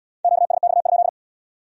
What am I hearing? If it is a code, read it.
Morse code, heard as a keyed tone, about 75 wpm, 700 Hz. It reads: QSX?